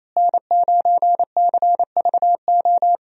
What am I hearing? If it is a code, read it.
Morse code, heard as a keyed tone, about 28 wpm, 700 Hz. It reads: N9CVO
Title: USA Call Signs - 5 Characters - Repeated 3x - 28wpm